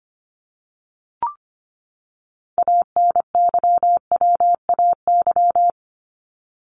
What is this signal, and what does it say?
Morse code, heard as a keyed tone, about 25 wpm, 700 Hz. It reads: ANYWAY
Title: Top 1501-1750 Words 25wpm